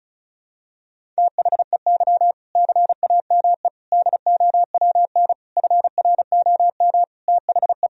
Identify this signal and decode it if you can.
Morse code, heard as a keyed tone, about 34 wpm, 700 Hz. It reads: THEY CAME DOWN FROM THE
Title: Sentences from Top 1000 Words - Review Part 1 - 35wpm